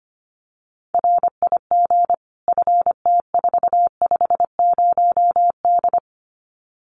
Morse code, heard as a keyed tone, about 25 wpm, 700 Hz. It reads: RIG FT450D
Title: Common QSO Phrases 25wpm